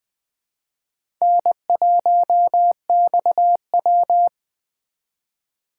Morse code, heard as a keyed tone, about 20 wpm, 700 Hz. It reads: N1XW
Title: USA Call Signs - 4 Characters - 4x Character Spacing - 20wpm